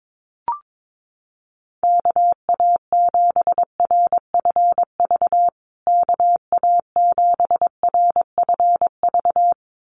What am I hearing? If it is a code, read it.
Morse code, heard as a keyed tone, about 22 wpm, 700 Hz. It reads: KA7RFV KA7RFV
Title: USA Call Signs Repeated 2x 22wpm